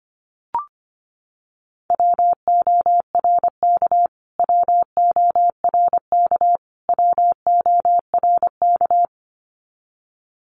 Morse code, heard as a keyed tone, about 25 wpm, 700 Hz. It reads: WORK WORK WORK
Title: Top 500 Words Repeated 25wpm